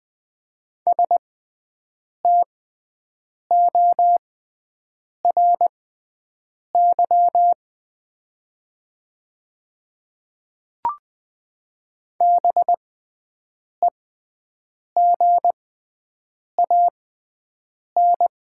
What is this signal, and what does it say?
Morse code, heard as a keyed tone, about 20 wpm, spaced as for 7 wpm, 700 Hz. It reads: STORY BEGAN
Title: 6x Character Spacing - Top 500 Words - 20wpm